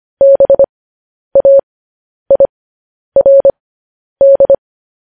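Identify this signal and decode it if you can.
Morse code, heard as a keyed tone, about 25 wpm, spaced as for 10 wpm, 550 Hz. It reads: BAIRD
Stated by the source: Morse code news headlines broadcast, opening with vvv.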